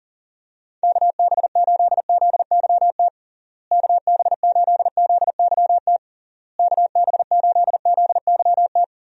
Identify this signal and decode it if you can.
Morse code, heard as a keyed tone, about 40 wpm, 700 Hz. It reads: KB8ZYT KB8ZYT KB8ZYT
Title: USA Call Signs - 6 Characters - Repeated 3x - 40wpm